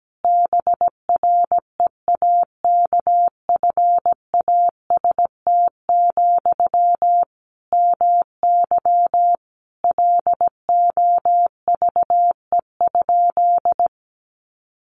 Morse code, heard as a keyed tone, about 17 wpm, 700 Hz. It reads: BREAKFAST, MY LOVE?
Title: Sentences from Top 1500 Words 17wpm